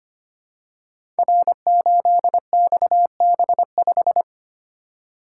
Morse code, heard as a keyed tone, about 25 wpm, 700 Hz. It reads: R8XB5
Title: Code Groups - 5 Letter:Numbers - 25wpm